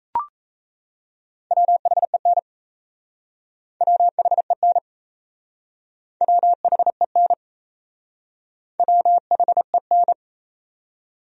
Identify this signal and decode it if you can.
Morse code, heard as a keyed tone, about 38 wpm, 700 Hz. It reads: WHEN WHEN WHEN WHEN